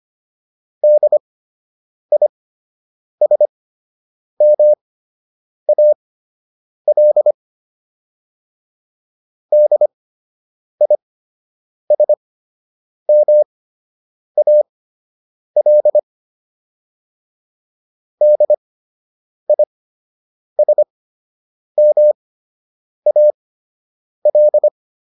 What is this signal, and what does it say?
Morse code, heard as a keyed tone, about 25 wpm, spaced as for 8 wpm, 600 Hz. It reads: DISMAL DISMAL DISMAL
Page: fundamental-practice-files